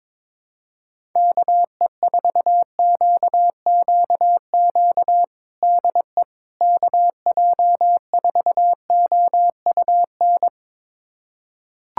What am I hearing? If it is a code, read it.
Morse code, heard as a keyed tone, about 22 wpm, 700 Hz. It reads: KE4QQQ DE KJ4OUN
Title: USA Callsign DE Callsign - 6 Characters - 22wpm